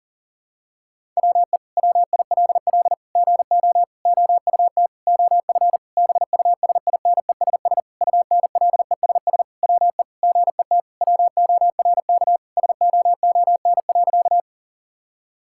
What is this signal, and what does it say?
Morse code, heard as a keyed tone, about 40 wpm, 690 Hz. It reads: WE WILL GO OUT OF BUSINESS UNLESS WE GET WORK SOON.
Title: Sentences from Top 1000 Words - Review Part 6 - 40wpm